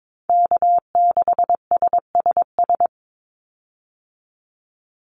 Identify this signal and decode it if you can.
Morse code, heard as a keyed tone, about 22 wpm, 700 Hz. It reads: K6SSS
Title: Call Signs Repeated 3x - 22wpm